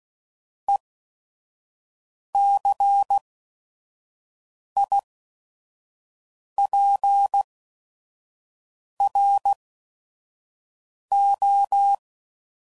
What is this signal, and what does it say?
Morse code, heard as a keyed tone, about 16 wpm, spaced as for 5 wpm, 800 Hz. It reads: ECIPRO
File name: CW Code Groups @5WPM / Technical Subjects / XC